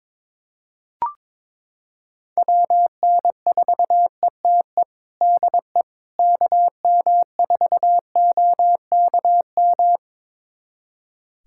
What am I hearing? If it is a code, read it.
Morse code, heard as a keyed tone, about 22 wpm, 700 Hz. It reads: WN4ETE DE KM4OKM
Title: USA Callsign DE Callsign 22wpm